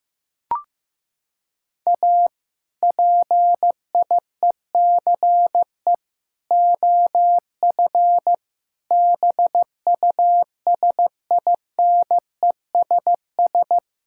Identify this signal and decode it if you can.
Morse code, heard as a keyed tone, about 15 wpm, 700 Hz. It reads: A PIECE OF BUSINESS
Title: Sentences from Top 1000 Words - Review Part 5 - 15wpm